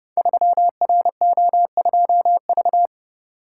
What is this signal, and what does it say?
Morse code, heard as a keyed tone, about 30 wpm, 700 Hz. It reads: 3RO2V
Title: Code Groups - 5 Letter:Numbers - 30wpm